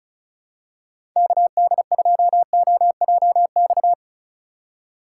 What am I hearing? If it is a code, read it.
Morse code, heard as a keyed tone, about 36 wpm, 700 Hz. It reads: KD2OJX
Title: USA Call Signs - 6 Characters - 3x Character Spacing - 35wpm